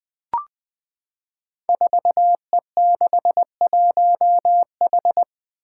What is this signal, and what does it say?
Morse code, heard as a keyed tone, about 20 wpm, 700 Hz. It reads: 4E61H
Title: Five Challenging Letter-Number Combinations 20wpm